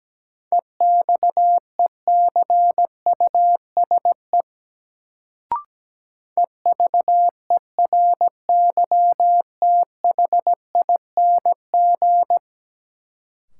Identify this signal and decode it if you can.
Morse code, heard as a keyed tone, about 17 wpm, 700 Hz. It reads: EXECUSE EVERYTHING
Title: Sentences from Top 700 Words - New Word Drill - 17wpm